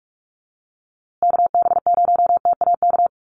Morse code, heard as a keyed tone, about 44 wpm, 700 Hz. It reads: KB0TAK
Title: USA Call Signs Repeated 3x 45wpm